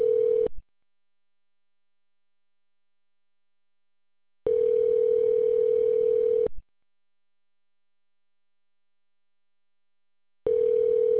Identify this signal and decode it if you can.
Ringback tone (North America), 440 + 480 Hz.